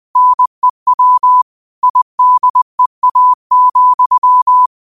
Morse code, heard as a keyed tone, about 20 wpm, 1000 Hz. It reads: NEW IDEA,
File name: qotdfiles / 20wpm